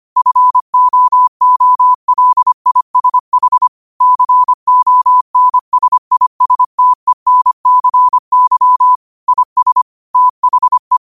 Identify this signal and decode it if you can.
Morse code, heard as a keyed tone, about 25 wpm, 1000 Hz. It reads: FOOLISH CONSISTENCY IS THE
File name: qotdfiles / 25wpm